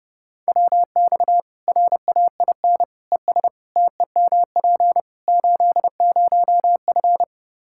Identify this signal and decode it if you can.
Morse code, heard as a keyed tone, about 32 wpm, 700 Hz. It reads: WX RAIN ES TEMP 80F
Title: K1USN SST QSO Practice 30wpm